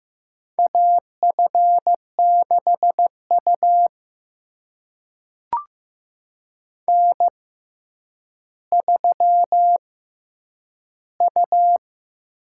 Morse code, heard as a keyed tone, about 15 wpm, spaced as for 9 wpm, 700 Hz. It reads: AF6U N3U
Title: USA Call Signs - 6x Character Spacing - 15wpm